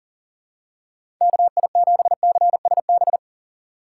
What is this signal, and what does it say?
Morse code, heard as a keyed tone, about 40 wpm, 700 Hz. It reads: KI7CSB
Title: USA Call Signs - 6 Characters - 2x Character Spacing - 40wpm